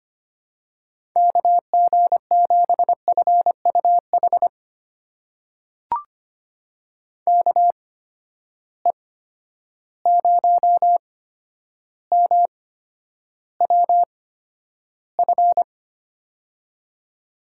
Morse code, heard as a keyed tone, about 25 wpm, spaced as for 11 wpm, 700 Hz. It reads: KG7FUH KE0MWF
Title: USA Call Signs - 6 Characters - 8x Character Spacing - 25wpm